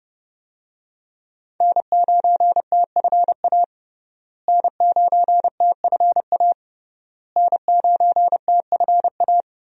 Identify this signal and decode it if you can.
Morse code, heard as a keyed tone, about 30 wpm, 700 Hz. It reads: N9TFA N9TFA N9TFA
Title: USA Call Signs - 5 Characters - Repeated 3x - 30wpm